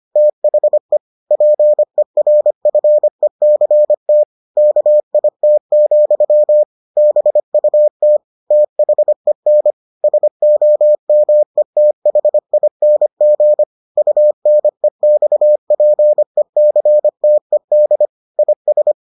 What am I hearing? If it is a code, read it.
Morse code, heard as a keyed tone, about 25 wpm, 600 Hz. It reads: THE PERFECT KIT, BUT THEN SOMETHING UNEXPECTED IS